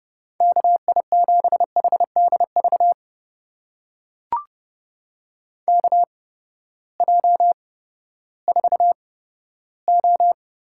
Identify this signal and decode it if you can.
Morse code, heard as a keyed tone, about 30 wpm, spaced as for 15 wpm, 700 Hz. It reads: KI7HDV KJ4O
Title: USA Call Signs - 6 Characters - 8x Character Spacing - 30wpm